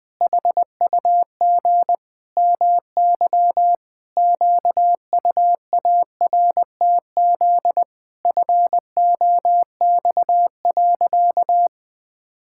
Morse code, heard as a keyed tone, about 20 wpm, 700 Hz. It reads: HUG MY QUARTZ FOX.
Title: Pangram Challenge v2 - 20wpm